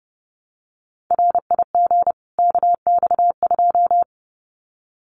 Morse code, heard as a keyed tone, about 30 wpm, 700 Hz. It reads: RIG KX2
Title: Common QSO Phrases 30wpm